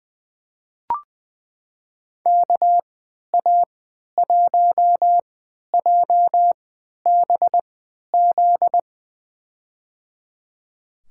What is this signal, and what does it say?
Morse code, heard as a keyed tone, about 21 wpm, spaced as for 11 wpm, 700 Hz. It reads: KA1JBZ
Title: USA Call Signs - 6 Characters - 3x Character Spacing - 20wpm